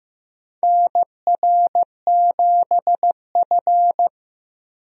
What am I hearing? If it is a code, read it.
Morse code, heard as a keyed tone, about 15 wpm, 700 Hz. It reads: NR7F
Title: USA Call Signs - 4 Characters - Repeated 2x - 15wpm